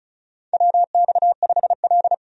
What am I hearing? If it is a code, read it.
Morse code, heard as a keyed tone, about 36 wpm, 700 Hz. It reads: WX5L
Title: USA Call Signs - 4 Characters - 35wpm